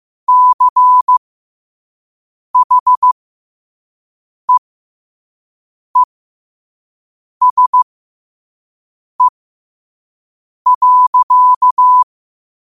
Morse code, heard as a keyed tone, about 15 wpm, spaced as for 5 wpm, 1000 Hz. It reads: CHEESE.